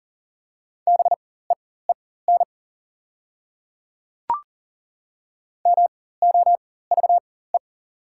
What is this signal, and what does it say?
Morse code, heard as a keyed tone, about 37 wpm, spaced as for 18 wpm, 700 Hz. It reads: BEEN MOVE
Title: 4x Character Spacing - Top 300 Words - 40wpm